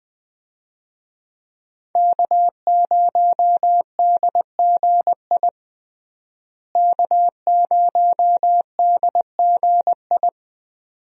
Morse code, heard as a keyed tone, about 20 wpm, 700 Hz. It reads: K0DGI K0DGI
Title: USA Call Signs - 5 Characters - Repeated 2x - 20wpm